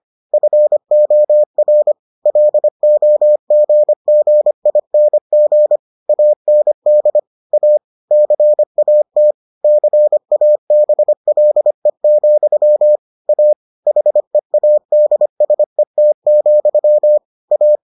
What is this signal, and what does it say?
Morse code, heard as a keyed tone, about 25 wpm, 600 Hz. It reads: FOR LOGGING AND A CAT CABLE, A HEADSET, A